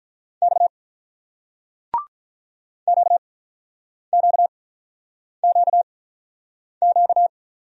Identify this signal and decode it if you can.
Morse code, heard as a keyed tone, about 48 wpm, spaced as for 9 wpm, 700 Hz. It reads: X QQQQ